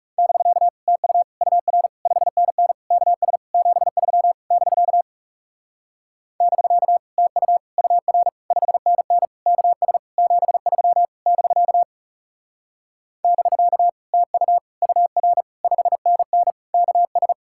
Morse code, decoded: <BK> TU UR 5NN KS 73 <BK> <BK> TU UR 5NN KS 73 <BK> <BK> TU UR 5NN KS

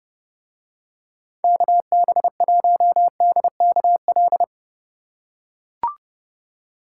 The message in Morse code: KB1DKL

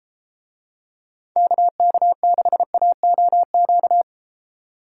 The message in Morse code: KK6AOQ

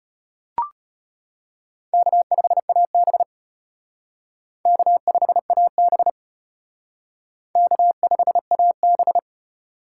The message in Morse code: K5AB K5AB K5AB

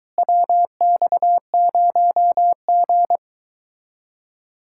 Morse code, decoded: WX0G

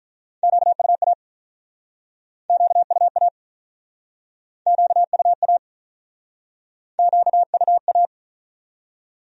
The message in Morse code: QUA QUA QUA QUA

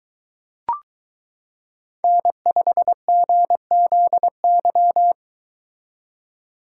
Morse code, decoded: N5GZY

